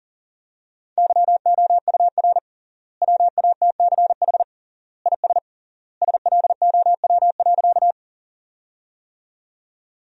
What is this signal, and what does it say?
Morse code, heard as a keyed tone, about 40 wpm, 700 Hz. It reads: YOUR WATCH IS SLOW.
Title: 3x Word-Spacing - Sentences from Top 400 Words - 40wpm